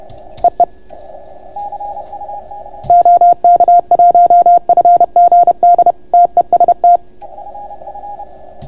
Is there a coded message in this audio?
Yes: Morse code from 0.4 to 7.0 s, heard as a keyed tone, about 31 wpm, 690 Hz. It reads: EE OK1FGD TEST